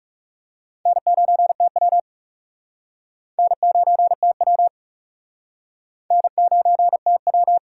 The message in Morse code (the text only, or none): N9TW N9TW N9TW